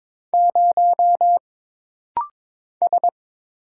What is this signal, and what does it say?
Morse code, heard as a keyed tone, about 22 wpm, 700 Hz. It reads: 0 S
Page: This rapid-fire timing is geared to push your ICR (Instant Character Recognition) skills.